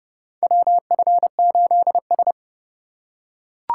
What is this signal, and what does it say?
Morse code, heard as a keyed tone, about 30 wpm, 700 Hz. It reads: WF8S
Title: USA Call Signs - 4 Characters - 6x Character Spacing - 30wpm